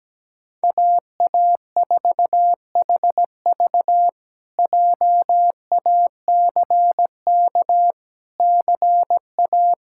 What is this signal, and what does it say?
Morse code, heard as a keyed tone, about 17 wpm, 700 Hz. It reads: AA4HV JACK CA
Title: K1USN SST QSO Practice - 17wpm